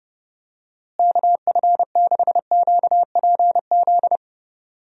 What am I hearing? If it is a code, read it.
Morse code, heard as a keyed tone, about 32 wpm, 700 Hz. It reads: KF6QPZ
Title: USA Call Signs - 6 Characters - Repeated 3x - 30wpm